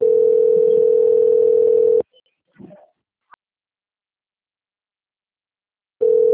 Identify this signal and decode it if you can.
Ringback tone (North America), 440 + 480 Hz.